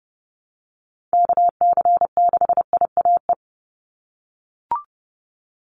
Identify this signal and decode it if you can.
Morse code, heard as a keyed tone, about 30 wpm, 700 Hz. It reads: KC6IAE
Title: USA Call Signs 30wpm